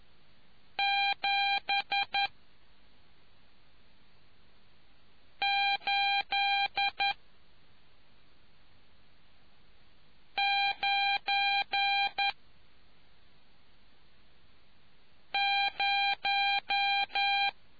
Morse code, heard as a keyed tone, about 11 wpm, spaced as for 3 wpm, 800 Hz. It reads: 7890